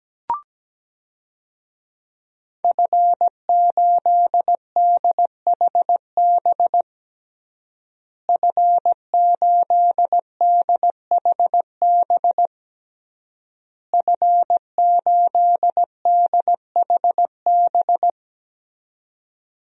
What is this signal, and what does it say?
Morse code, heard as a keyed tone, about 17 wpm, 700 Hz. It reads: F8DHB F8DHB F8DHB
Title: DXCC-100 Call Signs - 5 Characters - Repeated 3x - 17wpm